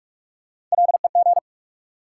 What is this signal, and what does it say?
Morse code, heard as a keyed tone, about 44 wpm, 700 Hz. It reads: LEG
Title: Three Letter Words - 45wpm